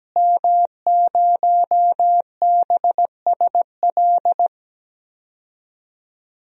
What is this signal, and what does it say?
Morse code, heard as a keyed tone, about 17 wpm, 700 Hz. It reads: M0BSL
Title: DXCC-100 Call Signs - 5 Characters - Repeated 2x - 17wpm